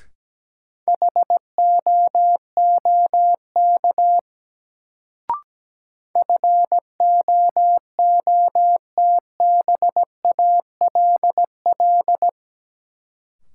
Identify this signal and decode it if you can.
Morse code, heard as a keyed tone, about 17 wpm, 700 Hz. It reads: HOOK FOOTBALL
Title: Sentences from Top 1600 Words - New Word Drill Alt - 17wpm